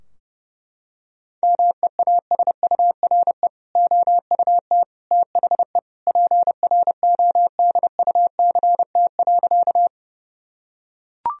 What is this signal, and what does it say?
Morse code, heard as a keyed tone, about 30 wpm, 700 Hz. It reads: MEASURE OUT THE PRODUCT.
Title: Sentences from Top 1000 Words - Review Part 2 - 30wpm